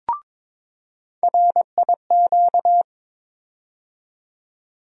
Morse code, heard as a keyed tone, about 22 wpm, 700 Hz. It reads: RIQ
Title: USA Call Sign Postfixes - 3 Letters 22wpm